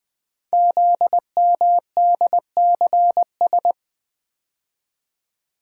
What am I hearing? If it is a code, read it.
Morse code, heard as a keyed tone, about 20 wpm, 700 Hz. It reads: ZMDCS
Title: Code Groups - 5 Letters - 20wpm